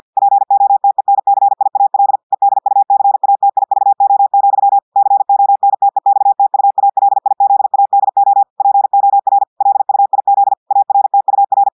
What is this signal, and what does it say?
Morse code, heard as a keyed tone, about 50 wpm, 800 Hz. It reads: POTENCIAL EDUCATIVO, CONTEXTUALIZADO POR LUIZ ARTUR
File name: audios / portugais / portugais texte en clair 1 vitesse 50 mots minutes